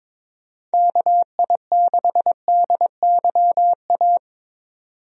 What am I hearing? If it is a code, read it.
Morse code, heard as a keyed tone, about 22 wpm, 700 Hz. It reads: KI6DYA
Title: USA Call Signs - 6 Characters - 22wpm